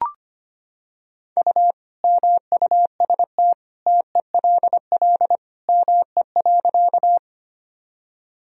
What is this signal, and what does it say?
Morse code, heard as a keyed tone, about 25 wpm, 700 Hz. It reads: U MUST TELL ME.
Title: Sentences from Top 200 Words and Abbreviations 25wpm